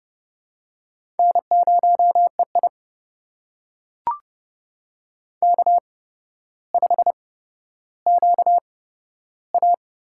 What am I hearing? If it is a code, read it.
Morse code, heard as a keyed tone, about 30 wpm, spaced as for 13 wpm, 700 Hz. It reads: N0EI K5QA
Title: USA Call Signs - 4 Characters - 8x Character Spacing - 30wpm